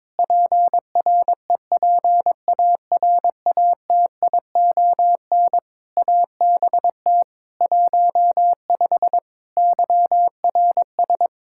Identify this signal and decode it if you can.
Morse code, heard as a keyed tone, about 22 wpm, 700 Hz. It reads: PREPARATION ABT 15 YRS
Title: K1USN SST QSO Practice 22wpm